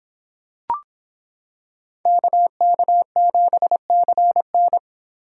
Morse code, heard as a keyed tone, about 26 wpm, 700 Hz. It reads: KK7CN